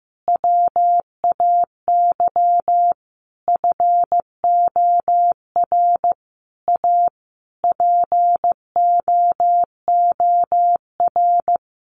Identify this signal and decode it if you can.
Morse code, heard as a keyed tone, about 15 wpm, 700 Hz. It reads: WAY FOR A POOR
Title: Sentences from Top 1300 Words 15wpm